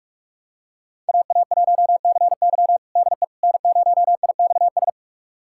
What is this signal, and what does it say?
Morse code, heard as a keyed tone, about 46 wpm, 700 Hz. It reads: AA1CY DE N0IXS
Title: USA Callsign DE Callsign - 5 Characters - 45wpm